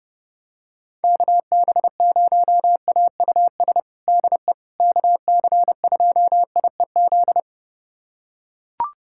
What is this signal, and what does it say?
Morse code, heard as a keyed tone, about 32 wpm, 700 Hz. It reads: KB0AUS DE KC2IEZ
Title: USA Callsign DE Callsign 30wpm